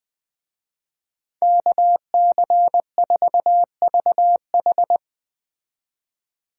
Morse code, heard as a keyed tone, about 20 wpm, 700 Hz. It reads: KC4VH